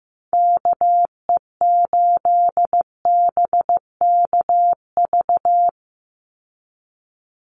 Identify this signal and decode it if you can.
Morse code, heard as a keyed tone, about 15 wpm, 700 Hz. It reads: KE8BKV
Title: USA Call Signs 15wpm